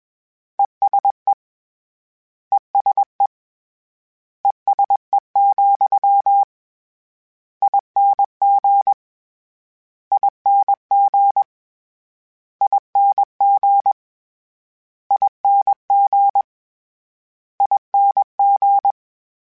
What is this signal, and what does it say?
Morse code, heard as a keyed tone, about 21 wpm, 800 Hz. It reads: ESE ESE ESE, ING ING ING ING ING
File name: General / General-mp3s / ESE ING @20